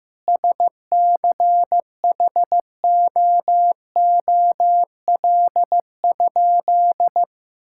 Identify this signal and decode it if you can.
Morse code, heard as a keyed tone, about 15 wpm, 700 Hz. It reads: SCHOOL?
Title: Sentences from Top 1000 Words - Review Part 5 - 15wpm